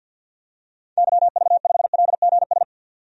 Morse code, heard as a keyed tone, about 47 wpm, 700 Hz. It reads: YV5LGS
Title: DXCC-100 Call Signs - 6 Characters - 50wpm